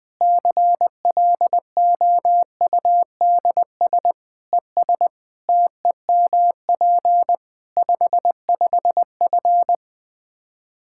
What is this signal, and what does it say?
Morse code, heard as a keyed tone, about 20 wpm, 700 Hz. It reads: CLOUDS ES TEMP 55F
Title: Practice QSO - Exchange 2 - Info - 20wpm